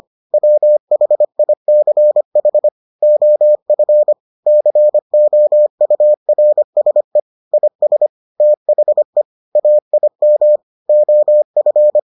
Morse code, heard as a keyed tone, about 25 wpm, 600 Hz. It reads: WHICH OF COURSE IS THE AIM OF